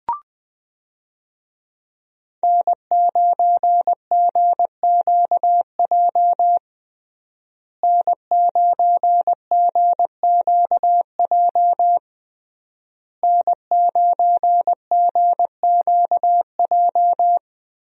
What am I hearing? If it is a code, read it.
Morse code, heard as a keyed tone, about 20 wpm, 700 Hz. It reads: N9GQJ N9GQJ N9GQJ
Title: USA Call Signs - 5 Characters - Repeated 3x - 20wpm